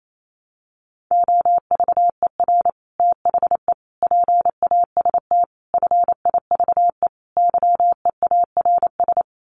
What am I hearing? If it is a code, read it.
Morse code, heard as a keyed tone, about 28 wpm, 690 Hz. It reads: OVER THE PAST FIVE YEARS
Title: Sets of 5 Words Encore 28wpm